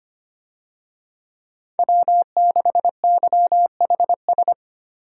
Morse code, heard as a keyed tone, about 25 wpm, 700 Hz. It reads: W6YHS
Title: USA Call Signs - 5 Characters - 25wpm